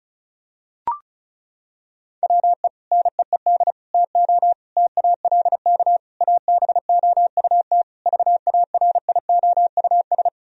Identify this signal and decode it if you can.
Morse code, heard as a keyed tone, about 34 wpm, 700 Hz. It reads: WE NEED TO TALK ABOUT VARIOUS
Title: Sentences from Top 1000 Words - Review Part 4 - 35wpm